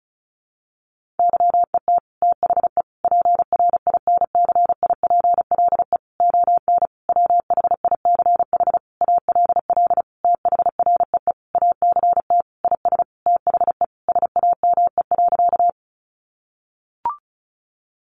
Morse code, heard as a keyed tone, about 34 wpm, 700 Hz. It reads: YET THE PRINCIPLE ON WHICH ALL THREE ACT IS THE SAME.
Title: Sentences from Top 1400 Words 35wpm